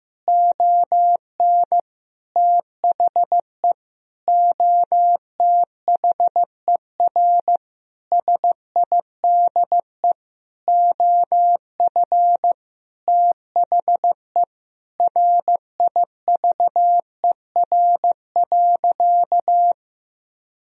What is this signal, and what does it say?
Morse code, heard as a keyed tone, about 16 wpm, 700 Hz. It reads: ON THE OTHER SIDE OF THE RIVER.
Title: Sentences from Top 1000 Words - Review Part 1 - 15wpm